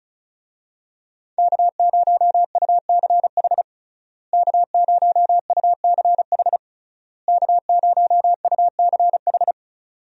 Morse code, heard as a keyed tone, about 36 wpm, 700 Hz. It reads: K0UCH K0UCH K0UCH
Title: USA Call Signs - 5 Characters - Repeated 3x - 35wpm